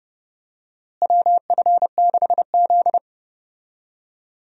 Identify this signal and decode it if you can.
Morse code, heard as a keyed tone, about 32 wpm, 700 Hz. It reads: WF6Z
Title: USA Call Signs - 4 Characters - Repeated 3x - 30wpm